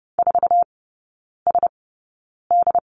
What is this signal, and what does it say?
Morse code, heard as a keyed tone, about 30 wpm, spaced as for 9 wpm, 700 Hz. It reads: 4SD
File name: Lesson 17e - Letter u - Three Characters 30wpm